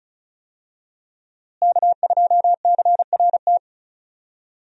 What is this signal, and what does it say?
Morse code, heard as a keyed tone, about 36 wpm, 700 Hz. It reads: K2CRT
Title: USA Call Signs - 5 Characters - Repeated 3x - 35wpm